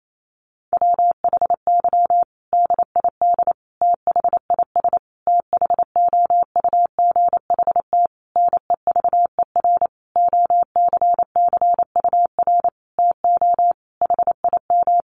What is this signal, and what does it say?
Morse code, heard as a keyed tone, about 28 wpm, 700 Hz. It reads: WHY DID THIS THOUGHT NEVER OCCUR TO HIM
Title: Sentences from Top 900 Words 28wpm